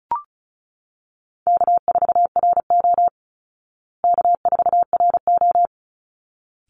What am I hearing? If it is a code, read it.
Morse code, heard as a keyed tone, about 34 wpm, 700 Hz. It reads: K4RO K4RO
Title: Call Signs Repeated 3x - 35wpm